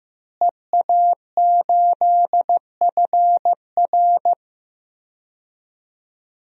Morse code, heard as a keyed tone, about 15 wpm, 700 Hz. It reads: EA8FR
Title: DXCC-100 Call Signs - 5 Characters - Repeated 3x - 15wpm